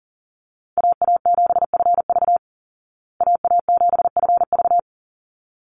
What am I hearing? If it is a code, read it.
Morse code, heard as a keyed tone, about 40 wpm, 700 Hz. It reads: AA7FV AA7FV
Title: Call Signs Repeated 3x - 40wpm